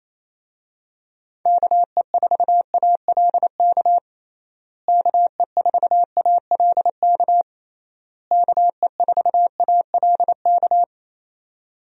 Morse code, heard as a keyed tone, about 27 wpm, 700 Hz. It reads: KE4ALK KE4ALK KE4ALK
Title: USA Call Signs - 6 Characters - Repeated 3x - 28wpm